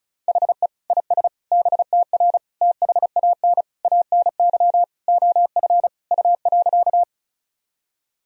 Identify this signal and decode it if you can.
Morse code, heard as a keyed tone, about 36 wpm, 700 Hz. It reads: HE IS BTR THAN ANY OF U.
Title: Sentences from Top 400 Words and Abbreviations 35wpm